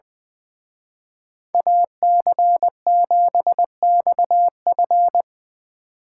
Morse code, heard as a keyed tone, about 24 wpm, 700 Hz. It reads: AC7XF
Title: Speed Racing - USA Call Signs 20wpm